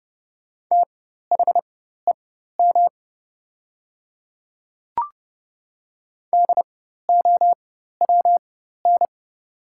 Morse code, heard as a keyed tone, about 30 wpm, spaced as for 14 wpm, 700 Hz. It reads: THEM DOWN
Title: 4x Character Spacing - Top 100 Words - 30wpm